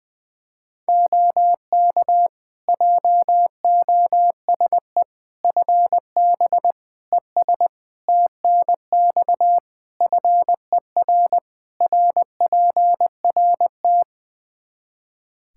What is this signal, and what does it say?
Morse code, heard as a keyed tone, about 21 wpm, 700 Hz. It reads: OK JOSE FB ES TNX FER RPRT
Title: K1USN SST QSO Practice 20wpm